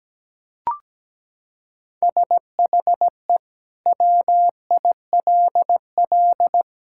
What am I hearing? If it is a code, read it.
Morse code, heard as a keyed tone, about 17 wpm, 700 Hz. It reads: SHE WILL
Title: Sentences from Top 1000 Words - Review Part 5 - 17wpm